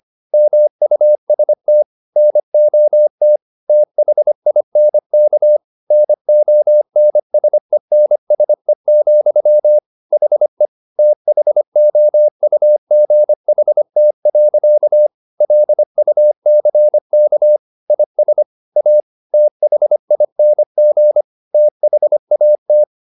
Morse code, heard as a keyed tone, about 25 wpm, 600 Hz. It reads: MUST NOT THINK NONSENSE, HE THOUGHT. LUCK IS A THING THAT